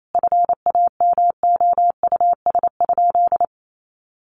Morse code, heard as a keyed tone, about 28 wpm, 700 Hz. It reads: FAMOUS?
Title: Sentences from Top 1500 Words 28wpm